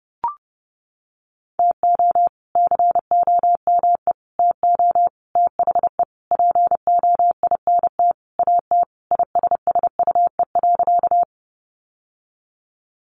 Morse code, heard as a keyed tone, about 30 wpm, 700 Hz. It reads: TO COME TO THE POINT AT ISSUE.
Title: Sentences from Top 900 Words 30wpm